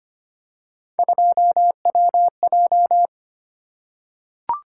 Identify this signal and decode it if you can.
Morse code, heard as a keyed tone, about 25 wpm, 700 Hz. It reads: 2WJ
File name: Three Letter-Number Combinations - 25wpm